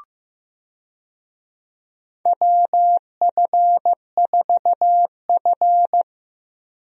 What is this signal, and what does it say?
Morse code, heard as a keyed tone, about 14 wpm, 700 Hz. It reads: WF4F
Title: USA Call Signs - 4 Characters - Repeated 3x - 15wpm